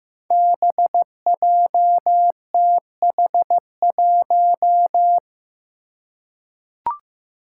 Morse code, heard as a keyed tone, about 15 wpm, 700 Hz. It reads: BJTH1
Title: Five Challenging Letter-Number Combinations 15wpm